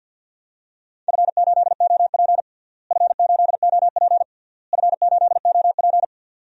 Morse code, heard as a keyed tone, about 51 wpm, 700 Hz. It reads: F8OP F8OP F8OP
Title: DXCC-100 Call Signs - 4 Characters - Repeated 3x - 50wpm